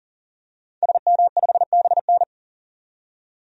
Morse code, heard as a keyed tone, about 40 wpm, 700 Hz. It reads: SM5BN
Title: DXCC-100 Call Signs - 5 Characters - 40wpm